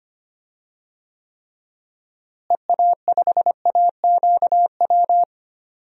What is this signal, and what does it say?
Morse code, heard as a keyed tone, about 25 wpm, 700 Hz. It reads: EA5AQW
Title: DXCC-100 Call Signs - 6 Characters - Repeated 3x - 25wpm